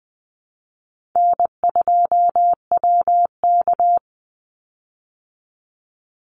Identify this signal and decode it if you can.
Morse code, heard as a keyed tone, about 20 wpm, 700 Hz. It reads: N2WK
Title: Easy Call Signs Repeated 2x - 20wpm